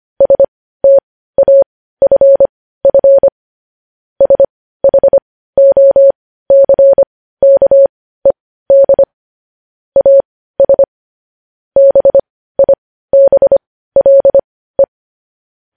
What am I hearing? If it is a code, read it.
Morse code, heard as a keyed tone, about 25 wpm, spaced as for 15 wpm, 550 Hz. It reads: STAFF SHOCKED AS BIBLE
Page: News Headlines in Morse Code at 15 WPM